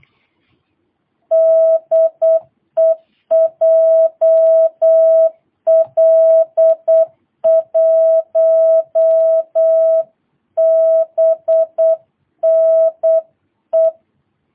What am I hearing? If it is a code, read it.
Morse code, heard as a keyed tone, about 8 wpm, 650 Hz. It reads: DE JL1BNE
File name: jh8cbh / 2022cw